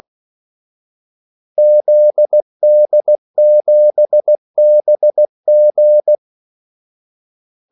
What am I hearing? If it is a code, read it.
Morse code, heard as a keyed tone, about 16 wpm, 600 Hz. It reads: ZD7BG
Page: Kaldesignaler 16 wpm